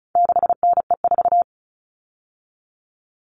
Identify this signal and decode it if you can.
Morse code, heard as a keyed tone, about 34 wpm, 700 Hz. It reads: 6NE4
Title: All Cut Numbers - 4-Digit 35wpm